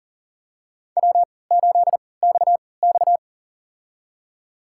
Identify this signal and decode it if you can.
Morse code, heard as a keyed tone, about 37 wpm, spaced as for 22 wpm, 700 Hz. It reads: W8XX
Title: USA Call Signs - 4 Characters - 3x Character Spacing - 40wpm